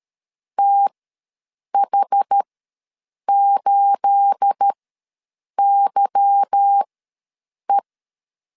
Morse code, heard as a keyed tone, about 13 wpm, spaced as for 7 wpm, 800 Hz. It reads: TH8YE
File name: LEZIONI / 9 28 ws 12wpm 3354